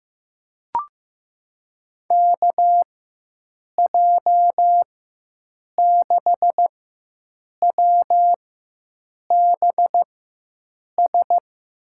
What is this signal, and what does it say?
Morse code, heard as a keyed tone, about 16 wpm, spaced as for 7 wpm, 700 Hz. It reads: KJ6WBS
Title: USA Call Signs - 6 Characters - 4x Character Spacing - 15wpm